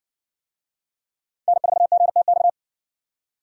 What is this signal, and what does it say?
Morse code, heard as a keyed tone, about 60 wpm, 700 Hz. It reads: N4GTX